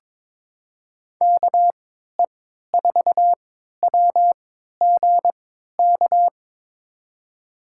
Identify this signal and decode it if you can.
Morse code, heard as a keyed tone, about 22 wpm, spaced as for 13 wpm, 700 Hz. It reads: KE4WGK
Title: USA Call Signs - 3x Character Spacing - 22wpm